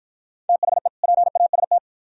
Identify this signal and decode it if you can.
Morse code, heard as a keyed tone, about 51 wpm, 700 Hz. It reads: THE PAST